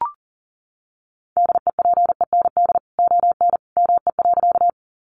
Morse code, heard as a keyed tone, about 40 wpm, 700 Hz. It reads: DEPEND ON ME.
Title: Sentences from Top 1300 Words 40wpm